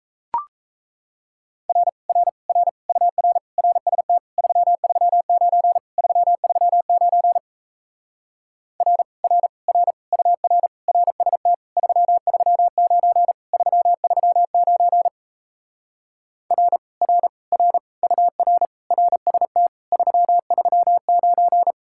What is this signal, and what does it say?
Morse code, heard as a keyed tone, about 37 wpm, 700 Hz. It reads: R R R UR RST 339 339 R R R UR RST 339 339 R R R UR RST 339